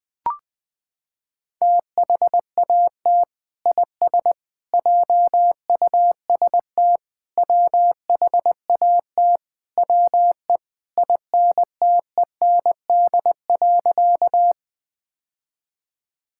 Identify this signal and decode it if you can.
Morse code, heard as a keyed tone, about 21 wpm, 700 Hz. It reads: THAT IS JUST WHAT WE INTEND.
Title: Sentences from Top 1600 Words 20wpm